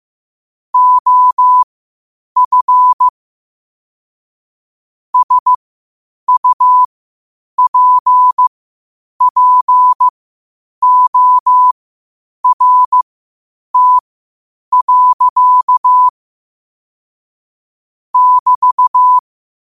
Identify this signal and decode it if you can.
Morse code, heard as a keyed tone, about 15 wpm, spaced as for 8 wpm, 1000 Hz. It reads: OF SUPPORT. <BT>